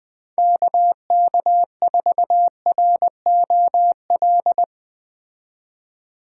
Morse code, heard as a keyed tone, about 20 wpm, 700 Hz. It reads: KK4ROL
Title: Speed Racing - USA 2x3 Call Signs 17wpm